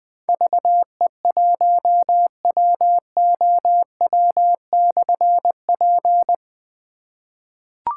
Morse code, heard as a keyed tone, about 20 wpm, 700 Hz. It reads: VE1WOW/P
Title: Difficult Call Signs 20wpm